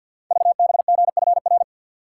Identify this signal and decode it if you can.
Morse code, heard as a keyed tone, about 50 wpm, 690 Hz. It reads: VBGFR
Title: Code Groups - 5 Letters - 50wpm